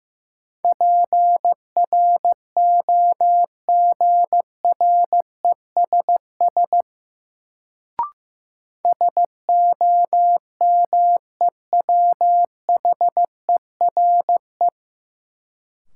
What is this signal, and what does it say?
Morse code, heard as a keyed tone, about 15 wpm, 700 Hz. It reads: PROGRESS SOMEWHERE
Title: Sentences from Top 1400 Words - New Word Drill - 15wpm